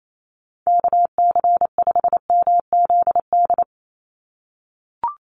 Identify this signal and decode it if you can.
Morse code, heard as a keyed tone, about 28 wpm, 700 Hz. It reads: KC5MZD
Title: USA Call Signs 28wpm